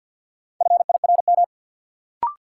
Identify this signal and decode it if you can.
Morse code, heard as a keyed tone, about 51 wpm, 700 Hz. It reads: FIRM
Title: Sentences from Top 1400 Words - New Word Drill - 50wpm